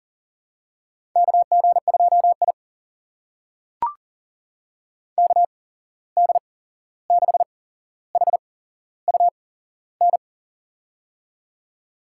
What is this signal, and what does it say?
Morse code, heard as a keyed tone, about 40 wpm, spaced as for 15 wpm, 700 Hz. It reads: KG2I KD6HUN
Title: USA Call Signs - 8x Character Spacing - 40wpm